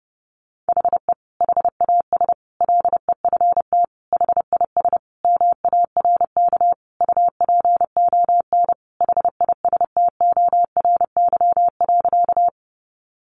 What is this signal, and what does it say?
Morse code, heard as a keyed tone, about 28 wpm, 700 Hz. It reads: HE HAS LEFT HIS MARK UPON HISTORY.
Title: Sentences from Top 1100 Words 30wpm